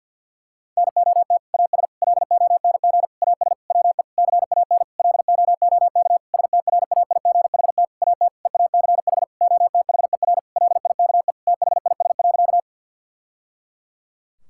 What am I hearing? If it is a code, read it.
Morse code, heard as a keyed tone, about 50 wpm, 700 Hz. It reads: NOT AS LONG AS WE CAN LOOK STRAIGHT AT EACH OTHER LIKE THIS.
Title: Sentences from Top 1000 Words - Review Part 6 - 50wpm